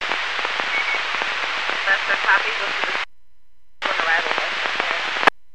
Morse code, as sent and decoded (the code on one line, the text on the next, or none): none